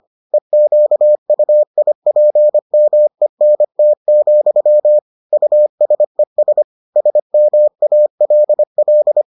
Morse code, heard as a keyed tone, about 25 wpm, 600 Hz. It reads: EQUIPMENT, USES SMALL